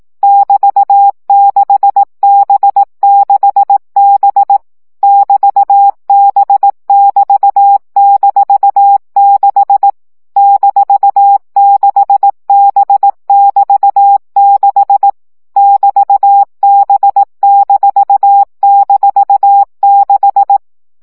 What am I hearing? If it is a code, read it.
Morse code, heard as a keyed tone, about 18 wpm, 800 Hz. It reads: <BT>6B6B <BT>B<BT>-6 -6B<BT>6 <BT>B--6
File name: CW gyakorlas b 6 egyenlo minusz 15 baud